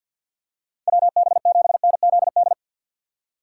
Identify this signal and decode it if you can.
Morse code, heard as a keyed tone, about 54 wpm, 700 Hz. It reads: WB7NZD
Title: USA Call Signs - 6 Characters - 4x Character Spacing - 50wpm